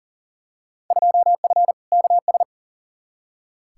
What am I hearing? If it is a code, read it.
Morse code, heard as a keyed tone, about 40 wpm, 700 Hz. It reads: 2F KS
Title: Field Day Exchanges Encore - 40wpm